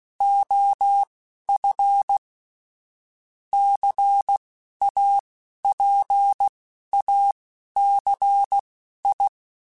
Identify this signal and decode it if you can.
Morse code, heard as a keyed tone, about 16 wpm, spaced as for 11 wpm, 800 Hz. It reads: OF CAPACI
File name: Technical Subjects / Technical Subjects-mp3s / XC @10